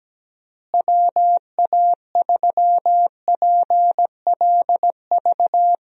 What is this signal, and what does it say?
Morse code, heard as a keyed tone, about 17 wpm, 700 Hz. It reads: WA3PLV
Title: USA Call Signs - 6 Characters - Repeated 2x - 17wpm